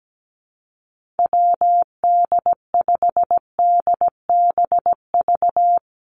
Morse code, heard as a keyed tone, about 17 wpm, 700 Hz. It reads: WD5DBV
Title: Call Signs Repeated 2x - 17wpm